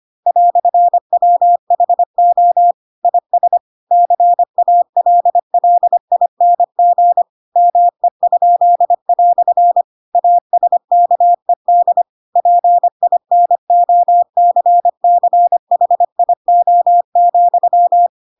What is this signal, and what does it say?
Morse code, heard as a keyed tone, about 25 wpm, 700 Hz. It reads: "WHO IS CALLING ME?" ASKED PINOCCHIO,
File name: Pinocchio / Pinocchio ch4 25 25 ik1hge